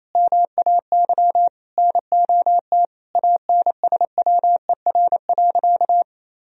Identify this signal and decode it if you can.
Morse code, heard as a keyed tone, about 28 wpm, 700 Hz. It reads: MAY NOT ANSWER.
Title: Sentences from Top 1000 Words - Review Part 4 - 28wpm